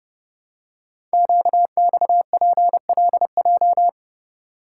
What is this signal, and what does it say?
Morse code, heard as a keyed tone, about 30 wpm, 690 Hz. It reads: QXPLJ
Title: Code Groups - 5 Letters - 30wpm